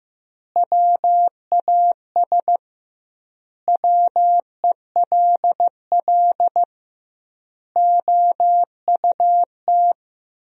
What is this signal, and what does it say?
Morse code, heard as a keyed tone, about 15 wpm, 700 Hz. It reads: WAS WELL OUT
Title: 2x Word-Spacing - Sentences from Top 400 Words - 15wpm